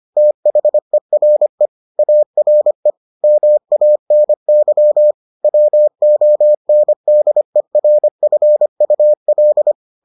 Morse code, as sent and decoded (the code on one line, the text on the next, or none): - .... .-. / .- .-. / -- .- -. -.-- / .-- --- -. -.. .-. ..-. ..- .-..
THERE ARE MANY WONDERFUL